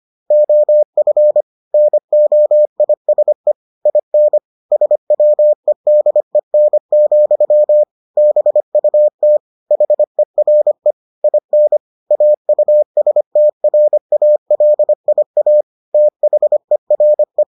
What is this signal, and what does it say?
Morse code, heard as a keyed tone, about 25 wpm, 600 Hz. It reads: OF NOISE IN SWEDEN, BUT HERE IN AUSTRALIA THERE